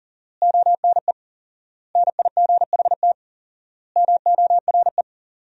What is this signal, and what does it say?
Morse code, heard as a keyed tone, about 40 wpm, 700 Hz. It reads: ONE NIGHT MORE
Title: 4x Word-Spacing - Sentences from Top 300 Words - 40wpm